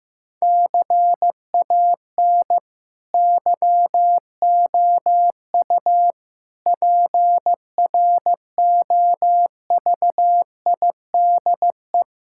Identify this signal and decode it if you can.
Morse code, heard as a keyed tone, about 15 wpm, 700 Hz. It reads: CAN YOU PROVIDE
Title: Sentences from Top 1000 Words - Review Part 5 - 15wpm